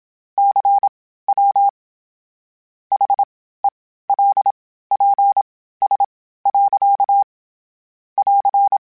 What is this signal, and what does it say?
Morse code, heard as a keyed tone, about 27 wpm, spaced as for 15 wpm, 800 Hz. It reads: CW HELPS. <AR>
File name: Misc / Misc-mp3s / QST QST @25